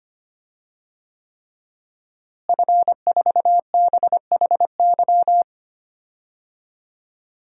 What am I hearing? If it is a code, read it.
Morse code, heard as a keyed tone, about 25 wpm, 700 Hz. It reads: F4BHY
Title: DXCC-100 Call Signs - 5 Characters - Repeated 3x - 25wpm